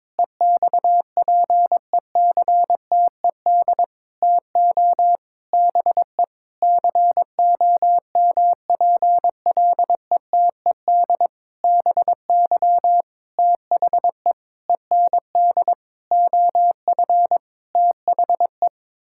Morse code, heard as a keyed tone, about 22 wpm, 700 Hz. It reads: EXPECTED TO BE COMPLETED BY THE END OF THE
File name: practice / 22wpm / Sentences with 8-Word Phrases - 22wpm